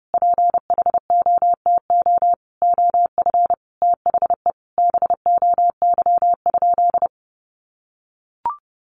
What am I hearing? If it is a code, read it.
Morse code, heard as a keyed tone, about 30 wpm, 700 Hz. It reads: PHOTO OF THE BOY?
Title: Sentences from Top 1200 Words 30wpm